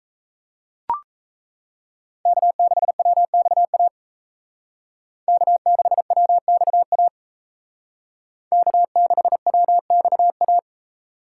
Morse code, heard as a keyed tone, about 35 wpm, 700 Hz. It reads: K6WXA K6WXA K6WXA